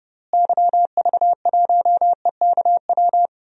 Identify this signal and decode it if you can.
Morse code, heard as a keyed tone, about 31 wpm, 700 Hz. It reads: YV1EKW
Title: DXCC-100 Call Signs - 6 Characters - Repeated 3x - 30wpm